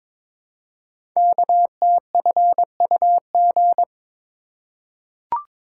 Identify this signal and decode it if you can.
Morse code, heard as a keyed tone, about 22 wpm, 700 Hz. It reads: KTFUG
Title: Code Groups - 5 Letters - 22wpm